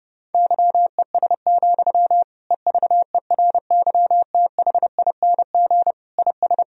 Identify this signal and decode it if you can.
Morse code, heard as a keyed tone, about 30 wpm, 700 Hz. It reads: YES, EVERYTHING IS
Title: Sentences from Top 1000 Words - Review Part 2 - 30wpm